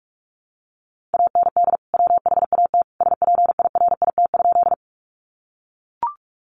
Morse code, heard as a keyed tone, about 44 wpm, 700 Hz. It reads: AND WHAT SPIRIT?
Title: Sentences from Top 1400 Words 45wpm